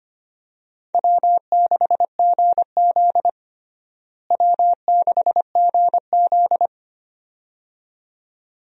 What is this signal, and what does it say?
Morse code, heard as a keyed tone, about 25 wpm, 700 Hz. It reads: W6GZ W6GZ
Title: USA Call Signs - 4 Characters - Repeated 2x - 25wpm